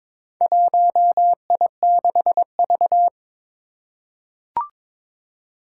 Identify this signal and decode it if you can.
Morse code, heard as a keyed tone, about 22 wpm, 700 Hz. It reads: 1I6V
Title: Four Challenging Letter-Number Combinations 22wpm